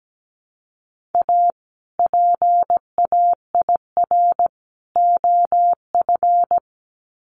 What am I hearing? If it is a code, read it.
Morse code, heard as a keyed tone, about 17 wpm, 700 Hz. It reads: A PAIR OF
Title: Sets of 3 Words 17wpm